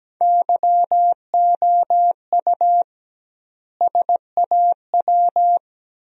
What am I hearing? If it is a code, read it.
Morse code, heard as a keyed tone, about 17 wpm, 700 Hz. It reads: YOU SAW
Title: 2x Word-Spacing - Sentences from Top 400 Words - 17wpm